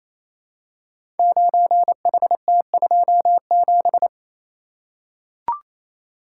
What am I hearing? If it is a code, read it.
Morse code, heard as a keyed tone, about 29 wpm, 700 Hz. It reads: 9HT27
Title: Code Groups - 5 Letter:Numbers - 28wpm